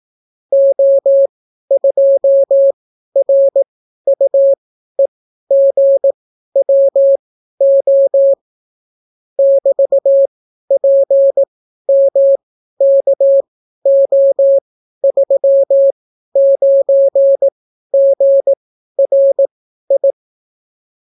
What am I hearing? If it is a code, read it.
Morse code, heard as a keyed tone, about 18 wpm, spaced as for 12 wpm, 550 Hz. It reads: O2RUEGWO <BT>PMKO39GRI